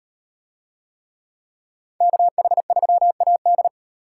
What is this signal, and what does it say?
Morse code, heard as a keyed tone, about 38 wpm, 700 Hz. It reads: KH3AD